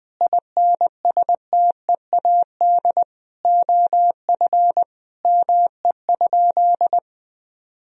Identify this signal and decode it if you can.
Morse code, heard as a keyed tone, about 21 wpm, 700 Hz. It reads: INSTEAD OF ME?
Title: Sentences from Top 1000 Words - Review Part 4 - 20wpm